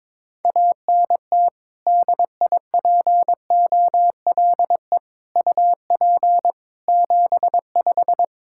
Morse code, heard as a keyed tone, about 22 wpm, 700 Hz. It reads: ANT DIPOLE UP 75
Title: K1USN SST QSO Practice 22wpm